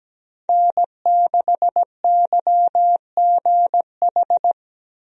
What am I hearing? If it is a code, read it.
Morse code, heard as a keyed tone, about 17 wpm, 700 Hz. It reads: N6YGH
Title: USA Call Signs - 5 Characters - Repeated 3x - 17wpm